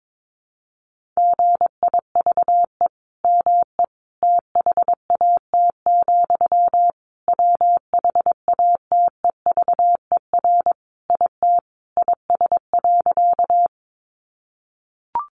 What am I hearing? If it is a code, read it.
Morse code, heard as a keyed tone, about 22 wpm, 700 Hz. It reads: GIVE ME THAT, WHATEVER IT IS.
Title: Sentences from Top 1100 Words 22wpm